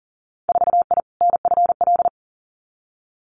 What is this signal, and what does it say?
Morse code, heard as a keyed tone, about 40 wpm, 700 Hz. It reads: 4I NFL
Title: Winter Field Day Exchanges 40wpm